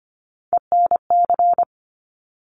Morse code, heard as a keyed tone, about 25 wpm, 700 Hz. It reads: ENC
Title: Top Three Letter Combinations in Words 25wpm